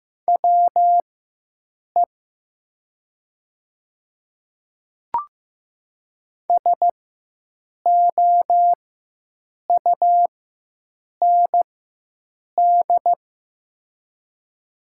Morse code, heard as a keyed tone, about 15 wpm, spaced as for 7 wpm, 700 Hz. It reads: WE SOUND
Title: 4x Character Spacing - Top 100 Words - 15wpm